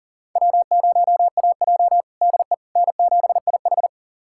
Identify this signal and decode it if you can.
Morse code, heard as a keyed tone, about 40 wpm, 700 Hz. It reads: W0AJ DE N7IH
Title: USA Callsign DE Callsign - 4 Characters - 40wpm